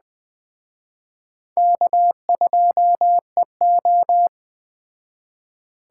Morse code, heard as a keyed tone, about 20 wpm, 700 Hz. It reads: K2EO
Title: Speed Racing - USA 1x2 Call Signs 17wpm